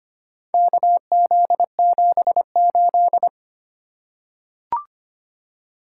Morse code, heard as a keyed tone, about 25 wpm, 700 Hz. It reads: KZ78
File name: practice / 25wpm / Four Letter-Number Combinations - 25wpm